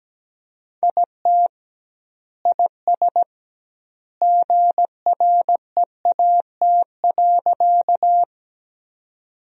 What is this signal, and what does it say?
Morse code, heard as a keyed tone, about 17 wpm, 700 Hz. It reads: IT IS GREAT.
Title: 2x Word-Spacing - Sentences from Top 200 Words - 17wpm